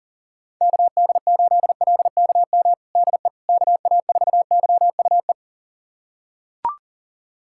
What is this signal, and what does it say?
Morse code, heard as a keyed tone, about 37 wpm, 700 Hz. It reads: KD8LKM DE KA4YUE
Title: USA Callsign DE Callsign - 6 Characters - 40wpm